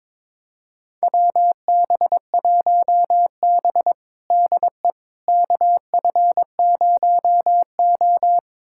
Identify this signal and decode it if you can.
Morse code, heard as a keyed tone, about 22 wpm, 700 Hz. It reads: WB1B DE KF0O
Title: USA Callsign DE Callsign - 4 Characters - 22wpm